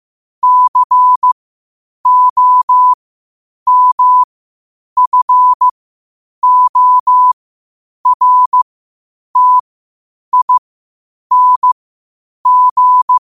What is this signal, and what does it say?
Morse code, heard as a keyed tone, about 15 wpm, spaced as for 8 wpm, 1000 Hz. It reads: COMFORTING
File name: qotdfiles / 8wpm